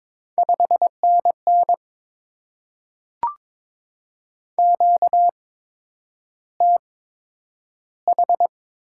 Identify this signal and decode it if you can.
Morse code, heard as a keyed tone, about 22 wpm, spaced as for 9 wpm, 700 Hz. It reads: 5NN QTH
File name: Top 90 QSO Elements - 8x Character Spacing - 22wpm